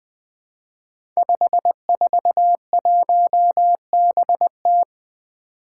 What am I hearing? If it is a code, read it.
Morse code, heard as a keyed tone, about 20 wpm, 700 Hz. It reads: 541BT
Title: Five Challenging Letter-Number Combinations 20wpm